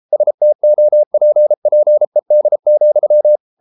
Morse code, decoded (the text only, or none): STOPPED,